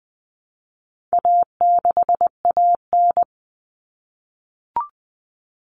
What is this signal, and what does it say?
Morse code, heard as a keyed tone, about 20 wpm, 700 Hz. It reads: A6AN
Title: Common Cut Numbers - 4-Digit 20wpm